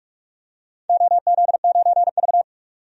Morse code, heard as a keyed tone, about 44 wpm, 700 Hz. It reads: OZ9V
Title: DXCC-100 Call Signs - 4 Characters - 45wpm